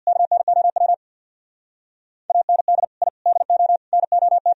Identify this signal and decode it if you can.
Morse code, heard as a keyed tone, about 64 wpm, 690 Hz. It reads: KNOW AND I DO NOT